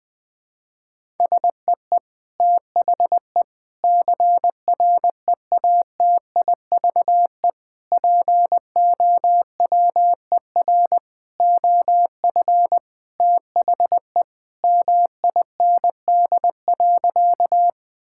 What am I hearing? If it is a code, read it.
Morse code, heard as a keyed tone, about 20 wpm, 700 Hz. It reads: SEE THE CREATIVE POWER OF THE MIND.
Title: Sentences from Top 1000 Words - Review Part 5 - 20wpm